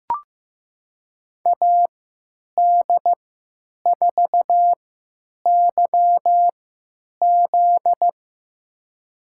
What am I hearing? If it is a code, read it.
Morse code, heard as a keyed tone, about 15 wpm, spaced as for 9 wpm, 700 Hz. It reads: AD4YZ
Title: USA Call Signs - 5 Characters - 3x Character Spacing - 15wpm